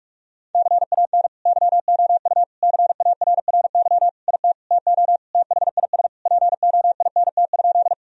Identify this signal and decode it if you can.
Morse code, heard as a keyed tone, about 42 wpm, 700 Hz. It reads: CAN YOU CARRY IT TO THIS POINT?
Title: Sentences from Top 1000 Words - Review Part 1 - 45wpm